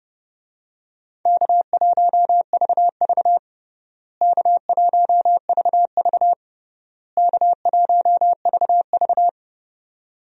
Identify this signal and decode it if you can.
Morse code, heard as a keyed tone, about 32 wpm, 700 Hz. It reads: K1VV K1VV K1VV
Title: USA Call Signs - 4 Characters - Repeated 3x - 30wpm